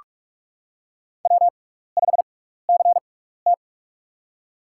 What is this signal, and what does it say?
Morse code, heard as a keyed tone, about 42 wpm, spaced as for 15 wpm, 700 Hz. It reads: W5CT
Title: USA Call Signs - 4 Characters - 6x Character Spacing - 45wpm